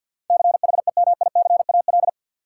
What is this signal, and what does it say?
Morse code, heard as a keyed tone, about 48 wpm, 700 Hz. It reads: CHEMICAL
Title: Sentences from Top 1200 Words - New Word Drill Alt - 50wpm